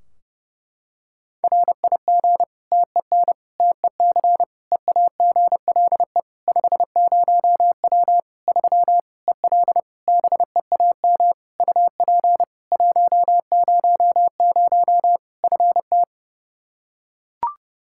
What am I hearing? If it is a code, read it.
Morse code, heard as a keyed tone, about 30 wpm, 700 Hz. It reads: RIG TEN TEC EAGLE 50W 3 EL BEAM UP 100 FT
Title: Practice QSO - Exchange 2 - Info - 30wpm